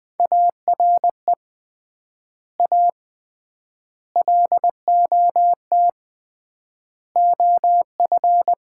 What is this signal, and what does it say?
Morse code, heard as a keyed tone, about 21 wpm, 700 Hz. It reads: ARE A LOT OF
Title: 3x Word-Spacing - Sets of 4 Words - 20wpm